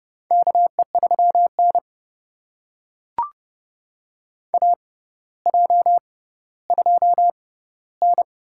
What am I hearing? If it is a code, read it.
Morse code, heard as a keyed tone, about 30 wpm, spaced as for 15 wpm, 700 Hz. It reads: KE3N AJ2N
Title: USA Call Signs - 4 Characters - 6x Character Spacing - 30wpm